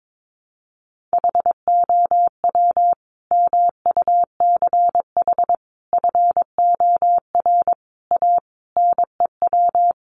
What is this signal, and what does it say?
Morse code, heard as a keyed tone, about 22 wpm, 700 Hz. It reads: HOW MUCH FOR A NEW
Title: Sentences from Top 1500 Words 22wpm